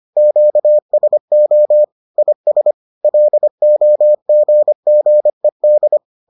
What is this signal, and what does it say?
Morse code, heard as a keyed tone, about 25 wpm, 600 Hz. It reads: QSO IS LOGGED